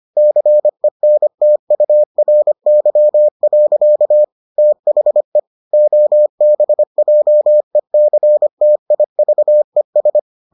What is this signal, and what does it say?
Morse code, heard as a keyed tone, about 25 wpm, 600 Hz. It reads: CENTURY. THE OBJECTIVES